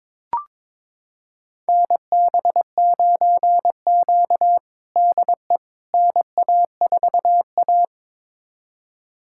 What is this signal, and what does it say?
Morse code, heard as a keyed tone, about 21 wpm, 700 Hz. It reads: NB9Q DE NA4A
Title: USA Callsign DE Callsign - 4 Characters - 22wpm